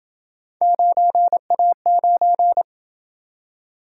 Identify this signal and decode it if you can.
Morse code, heard as a keyed tone, about 28 wpm, 700 Hz. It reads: 9A9